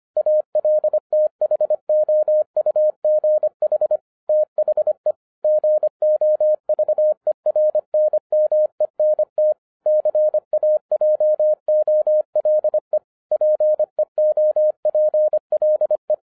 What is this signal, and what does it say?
Morse code, heard as a keyed tone, about 25 wpm, 610 Hz. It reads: ALTHOUGH THE GOVERNMENT CAJOLE PEOPLE
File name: 2011-07-mp3 / n39 THE 3.11 2 25wpm